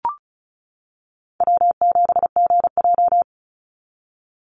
Morse code, heard as a keyed tone, about 36 wpm, 700 Hz. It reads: W7GJ
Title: NA SOTA Chaser Callsigns 35wpm